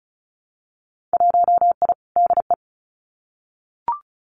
Morse code, heard as a keyed tone, about 34 wpm, 700 Hz. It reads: 1I DE
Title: Winter Field Day Exchanges Encore 35wpm